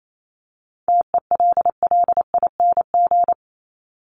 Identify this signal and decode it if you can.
Morse code, heard as a keyed tone, about 28 wpm, 700 Hz. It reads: TELLING
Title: ING Suffix - Top 100 Words with Base Words 28wpm